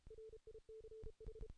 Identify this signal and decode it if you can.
Morse code, heard as a keyed tone, about 32 wpm, 440 Hz. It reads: RICH
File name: 2016 JAN CW NAQP